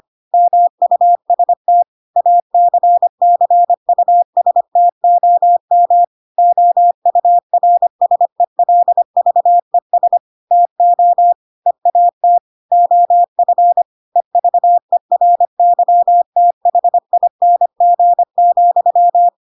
Morse code, heard as a keyed tone, about 25 wpm, 700 Hz. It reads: MUST ACCUSTOM OURSELVES TO EAT OF EVERYTHING,
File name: Pinocchio / Pinocchio ch7 25 25 ik1hge